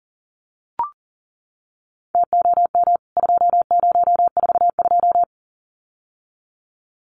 Morse code, heard as a keyed tone, about 40 wpm, 700 Hz. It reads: TOM 2042
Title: CWT Exchange 40wpm